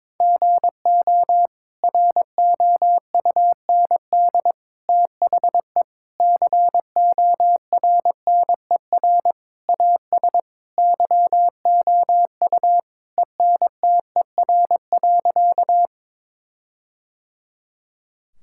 Morse code, heard as a keyed tone, about 22 wpm, 700 Hz. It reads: GO ROUND THE CORNER AS YOU ENTER.
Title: Sentences from Top 1600 Words 22wpm